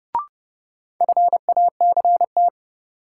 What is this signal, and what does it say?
Morse code, heard as a keyed tone, about 31 wpm, 700 Hz. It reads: FACT
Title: Sentences from Top 500 Words - New Word Drill - 30wpm